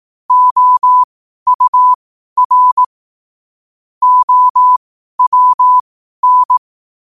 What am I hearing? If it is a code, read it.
Morse code, heard as a keyed tone, about 18 wpm, spaced as for 12 wpm, 1000 Hz. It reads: OUR OWN
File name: qotdfiles / 12wpm